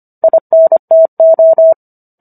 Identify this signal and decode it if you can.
Morse code, heard as a keyed tone, about 25 wpm, 650 Hz. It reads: INTO